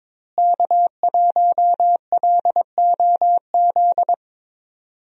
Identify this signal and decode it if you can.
Morse code, heard as a keyed tone, about 22 wpm, 700 Hz. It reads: K1LOZ
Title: USA Call Signs - 5 Characters - Repeated 3x - 22wpm